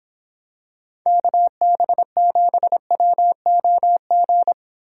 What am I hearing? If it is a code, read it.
Morse code, heard as a keyed tone, about 26 wpm, 700 Hz. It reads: KB7WOG